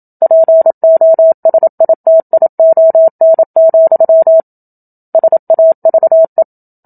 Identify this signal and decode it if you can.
Morse code, heard as a keyed tone, about 27 wpm, 650 Hz. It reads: POSITION, SAVE
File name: Book Two / 01-Chapter 1 - 27 WPM